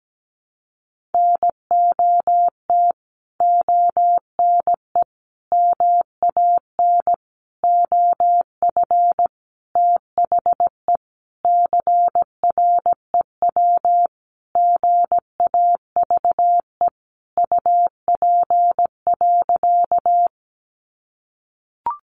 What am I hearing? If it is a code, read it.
Morse code, heard as a keyed tone, about 17 wpm, 700 Hz. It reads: NOT ONE MAN OF THE CREW GAVE UP.
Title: Sentences from Top 1500 Words 17wpm